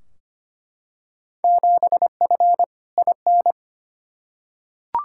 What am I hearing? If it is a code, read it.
Morse code, heard as a keyed tone, about 25 wpm, 700 Hz. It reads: 7F IN
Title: Field Day Exchanges Encore - 25wpm